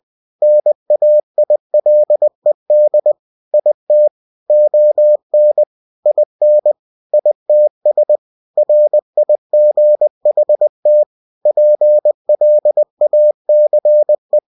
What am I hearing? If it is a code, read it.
Morse code, heard as a keyed tone, about 20 wpm, 600 Hz. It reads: NAILED IT ON IN ITS RIGHT PLACE
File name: cw-books / winnie-the-pooh / winnie-20wpm-page-0032